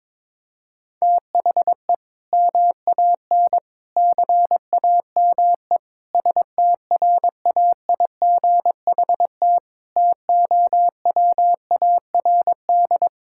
Morse code, heard as a keyed tone, about 22 wpm, 700 Hz. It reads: THE MAN CAME STRAIGHT TOWARD
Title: Sentences from Top 1000 Words - Review Part 6 - 22wpm